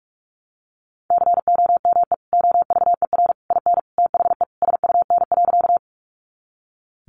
Morse code, heard as a keyed tone, about 44 wpm, 700 Hz. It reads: COME OVER IN THE SUN.
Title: Sentences from Top 300 Words 45wpm